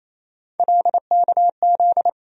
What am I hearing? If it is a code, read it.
Morse code, heard as a keyed tone, about 28 wpm, 700 Hz. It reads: LKZ
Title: USA Call Sign Postfixes - 3 Letters 28wpm